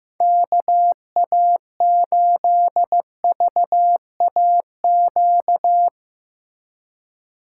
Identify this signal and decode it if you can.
Morse code, heard as a keyed tone, about 17 wpm, 700 Hz. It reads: KA8VAQ
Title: Speed Racing - USA 2x3 Call Signs 15wpm